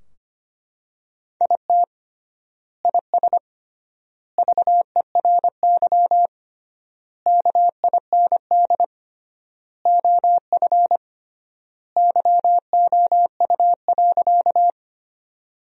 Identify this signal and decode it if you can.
Morse code, heard as a keyed tone, about 25 wpm, 700 Hz. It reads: IT IS VERY KIND OF YOU.
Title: 3x Word-Spacing - Sentences from Top 200 Words - 25wpm